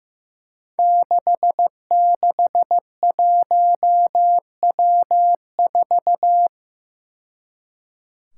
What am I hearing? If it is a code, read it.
Morse code, heard as a keyed tone, about 15 wpm, 700 Hz. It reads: 661W4
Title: Five Challenging Letter-Number Combinations 15wpm